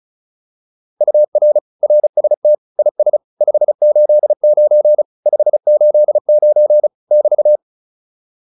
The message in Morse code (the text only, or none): UR RST IS 589 589 <BT>